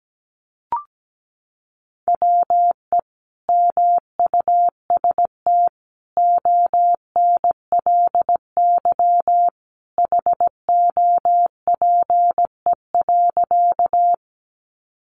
Morse code, heard as a keyed tone, about 17 wpm, 700 Hz. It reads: WE MUST ONLY HOPE.
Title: Sentences from Top 1000 Words 17wpm